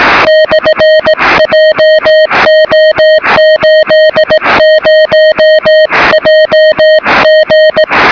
Morse code, heard as a keyed tone, about 18 wpm, 620 Hz, with noise. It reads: /JO80JG
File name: sm7lcb / 1824z sp6gwb 23cm